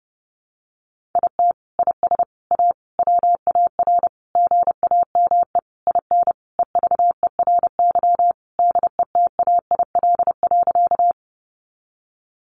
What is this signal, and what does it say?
Morse code, heard as a keyed tone, about 30 wpm, 700 Hz. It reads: IT IS A WAR GAME IN EVERY DETAIL.
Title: Sentences from Top 1500 Words 30wpm